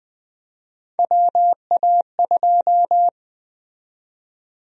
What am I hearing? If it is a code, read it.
Morse code, heard as a keyed tone, about 20 wpm, 700 Hz. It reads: WA2
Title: USA Call Sign Prefixes - Repeated 2x - 20wpm